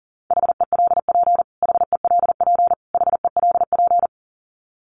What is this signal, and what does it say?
Morse code, heard as a keyed tone, about 40 wpm, 700 Hz. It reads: HELP HELP HELP
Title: Top 300 Words Repeated 40wpm